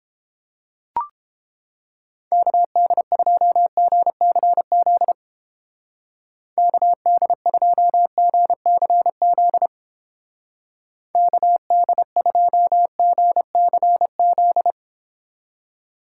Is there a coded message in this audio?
Yes: Morse code from 2.3 to 14.7 s, heard as a keyed tone, about 30 wpm, 700 Hz. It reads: KD2GCZ KD2GCZ KD2GCZ